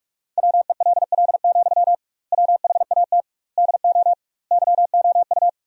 Morse code, heard as a keyed tone, about 46 wpm, 700 Hz. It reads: WELL, WHAT DO YOU
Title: 2x Word-Spacing - Sentences from Top 200 Words - 45wpm